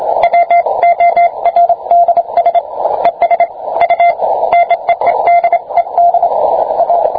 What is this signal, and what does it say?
Morse code, heard as a keyed tone, about 25 wpm, 690 Hz, with radio noise. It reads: WORDS ES U NEEDED A